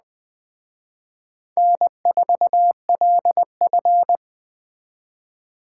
Morse code, heard as a keyed tone, about 20 wpm, 700 Hz. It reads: N4LF
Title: Speed Racing - USA 1x2 Call Signs 17wpm